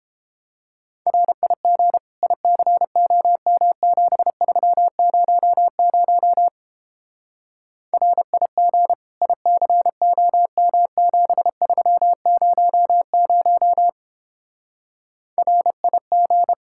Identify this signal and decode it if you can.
Morse code, heard as a keyed tone, about 29 wpm, 700 Hz. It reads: RIG ICOM7300 RIG ICOM7300 RIG